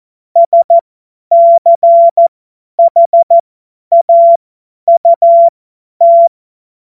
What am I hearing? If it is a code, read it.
Morse code, heard as a keyed tone, about 14 wpm, spaced as for 10 wpm, 680 Hz. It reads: SCHAUT
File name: Morsekurs / Witze / Witz 17.Oktober 2023 50 F70